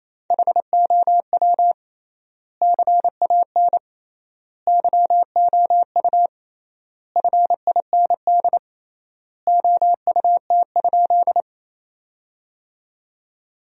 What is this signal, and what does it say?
Morse code, heard as a keyed tone, about 28 wpm, 700 Hz. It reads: HOW CAN YOU FIND OUT?
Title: 3x Word-Spacing - Sentences from Top 200 Words - 28wpm